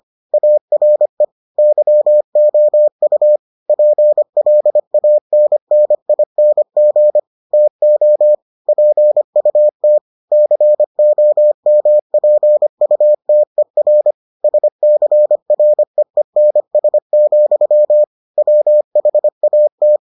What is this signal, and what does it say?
Morse code, heard as a keyed tone, about 25 wpm, 600 Hz. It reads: ARE YOU PLANNING TO PUT COMPUTER SCREENS, WHAT